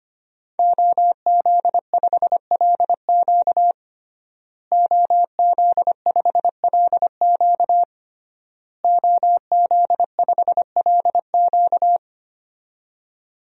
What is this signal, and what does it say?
Morse code, heard as a keyed tone, about 25 wpm, 700 Hz. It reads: OZ5LQ OZ5LQ OZ5LQ
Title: DXCC-100 Call Signs - 5 Characters - Repeated 3x - 25wpm